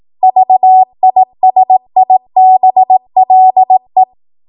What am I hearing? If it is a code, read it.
Morse code, heard as a keyed tone, about 18 wpm, 750 Hz. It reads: VISIBLE
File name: Morse / 18WPM